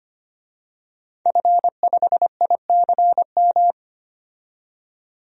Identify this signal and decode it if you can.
Morse code, heard as a keyed tone, about 25 wpm, 700 Hz. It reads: F5ICM
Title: DXCC-100 Call Signs - 5 Characters - 25wpm